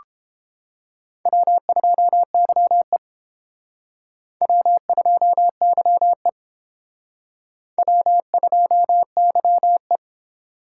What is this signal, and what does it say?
Morse code, heard as a keyed tone, about 30 wpm, 700 Hz. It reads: W2YE W2YE W2YE